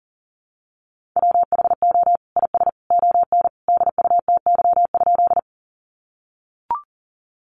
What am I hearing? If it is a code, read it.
Morse code, heard as a keyed tone, about 40 wpm, 700 Hz. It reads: WHO IS ON DUTY?
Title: Sentences from Top 1500 Words 40wpm